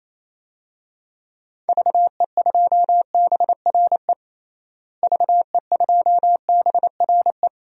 Morse code, heard as a keyed tone, about 28 wpm, 700 Hz. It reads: VE2BRE VE2BRE
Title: DXCC-100 Call Signs - 6 Characters - Repeated 3x - 28wpm